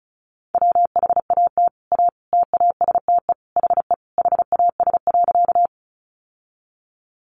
Morse code, heard as a keyed tone, about 34 wpm, 700 Hz. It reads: WHAT A TASTE HE HAS.
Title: Sentences from Top 1100 Words 35wpm